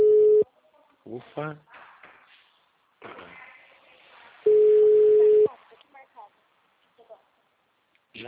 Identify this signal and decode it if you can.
Ringback tone (Europe, 425 Hz), with speech.
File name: out- -2011- - - .13593